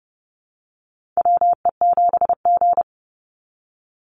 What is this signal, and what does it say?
Morse code, heard as a keyed tone, about 30 wpm, 700 Hz. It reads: WE7G
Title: Call Signs Repeated 3x - 30wpm